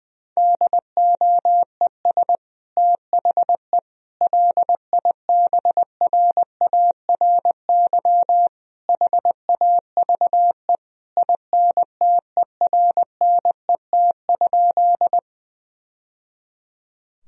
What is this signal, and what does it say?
Morse code, heard as a keyed tone, about 20 wpm, 700 Hz. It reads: DOES THE LIBRARY HAVE INTERNET?
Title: Sentences from Top 1000 Words - Review Part 4 - 20wpm